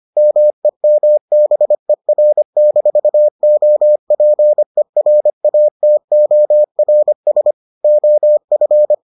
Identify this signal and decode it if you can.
Morse code, heard as a keyed tone, about 25 wpm, 600 Hz. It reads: MEMBER-OPERATORS OF